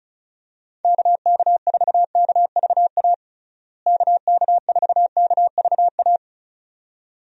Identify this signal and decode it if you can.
Morse code, heard as a keyed tone, about 36 wpm, 700 Hz. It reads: KK4KVA KK4KVA
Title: USA Call Signs - 6 Characters - Repeated 2x - 35wpm